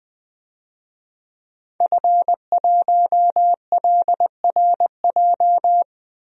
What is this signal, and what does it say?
Morse code, heard as a keyed tone, about 20 wpm, 700 Hz. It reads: F1LRJ
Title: DXCC-100 Call Signs - 5 Characters - 20wpm